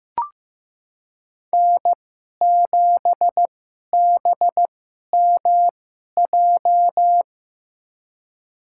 Morse code, heard as a keyed tone, about 15 wpm, spaced as for 11 wpm, 700 Hz. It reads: N7BMJ
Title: USA Call Signs - 5 Characters - 2x Character Spacing - 15wpm